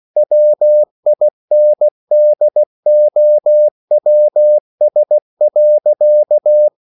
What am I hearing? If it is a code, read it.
Morse code, heard as a keyed tone, about 16 wpm, 600 Hz. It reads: WINDOWS.